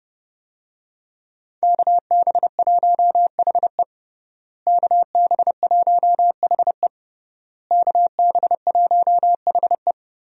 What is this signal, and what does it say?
Morse code, heard as a keyed tone, about 30 wpm, 700 Hz. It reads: KB1HE KB1HE KB1HE
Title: USA Call Signs - 5 Characters - Repeated 3x - 30wpm